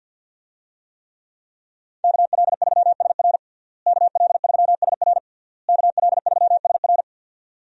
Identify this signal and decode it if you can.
Morse code, heard as a keyed tone, about 52 wpm, 700 Hz. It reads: KL3SR KL3SR KL3SR
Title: USA Call Signs - 5 Characters - Repeated 3x - 50wpm